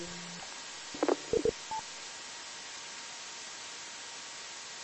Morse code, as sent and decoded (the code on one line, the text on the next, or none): none